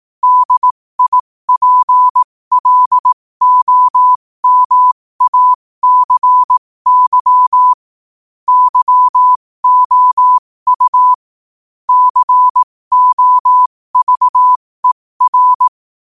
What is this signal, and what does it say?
Morse code, heard as a keyed tone, about 18 wpm, spaced as for 15 wpm, 1000 Hz. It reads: DIPLOMACY YOU COVER
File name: qotdfiles / 15wpm